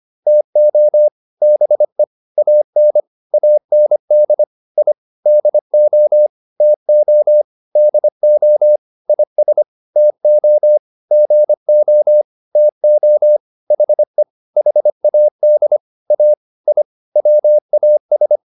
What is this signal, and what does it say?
Morse code, heard as a keyed tone, about 25 wpm, 600 Hz. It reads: TO BE AN AND I DO TO DO IS TO GO TO HE HAD A I WAS